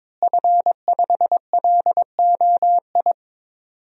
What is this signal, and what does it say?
Morse code, heard as a keyed tone, about 22 wpm, 700 Hz. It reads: F5LOI
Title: DXCC-100 Call Signs - 5 Characters - Repeated 3x - 22wpm